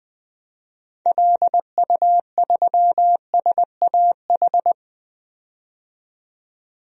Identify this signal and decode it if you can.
Morse code, heard as a keyed tone, about 20 wpm, 700 Hz. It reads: LU3SAH
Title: DXCC-100 Call Signs - 6 Characters - 20wpm